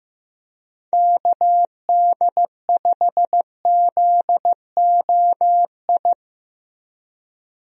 Morse code, heard as a keyed tone, about 15 wpm, 700 Hz. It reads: KD5ZOI
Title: USA Call Signs - 6 Characters - Repeated 3x - 15wpm